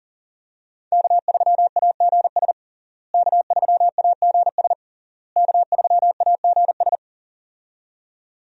Morse code, heard as a keyed tone, about 37 wpm, 700 Hz. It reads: K3AGS K3AGS K3AGS
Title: USA Call Signs - 5 Characters - Repeated 3x - 40wpm